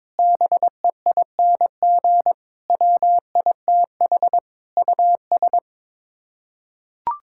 Morse code, heard as a keyed tone, about 21 wpm, 700 Hz. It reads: BEING WITH US
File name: Sets of 9 Words - 22wpm